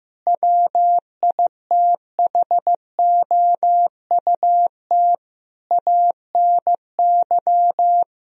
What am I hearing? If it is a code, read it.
Morse code, heard as a keyed tone, about 15 wpm, 700 Hz. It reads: WITHOUT ANY
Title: Sentences from Top 1000 Words - Review Part 4 - 15wpm